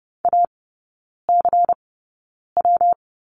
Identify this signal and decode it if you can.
Morse code, heard as a keyed tone, about 30 wpm, spaced as for 9 wpm, 700 Hz. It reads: ACW
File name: Lesson 19e - Letter w - Three Characters 30wpm